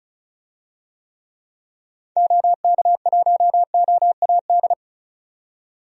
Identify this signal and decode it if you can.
Morse code, heard as a keyed tone, about 34 wpm, 700 Hz. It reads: OK1OAD
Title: DXCC-100 Call Signs - 6 Characters - 35wpm